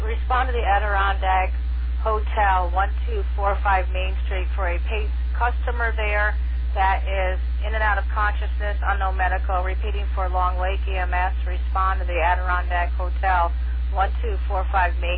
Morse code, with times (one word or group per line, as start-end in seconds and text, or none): none